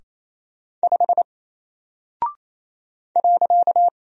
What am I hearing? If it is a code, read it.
Morse code, heard as a keyed tone, about 28 wpm, 700 Hz. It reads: 5 .
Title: Single Character (Letter:Number:Punctuation:Prosign) Alt 28wpm